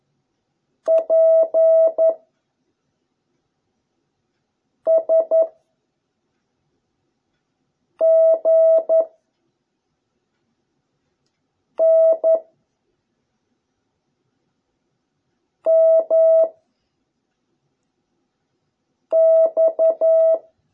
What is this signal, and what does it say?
Morse code, heard as a keyed tone, about 11 wpm, 660 Hz. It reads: P S G N M X